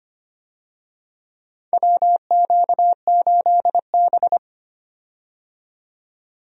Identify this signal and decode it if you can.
Morse code, heard as a keyed tone, about 25 wpm, 700 Hz. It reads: WQ8B
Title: USA Call Signs - 4 Characters - 25wpm